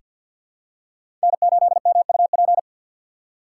Morse code, heard as a keyed tone, about 48 wpm, 700 Hz. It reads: N8MUP
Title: USA Call Signs - 5 Characters - 6x Character Spacing - 50wpm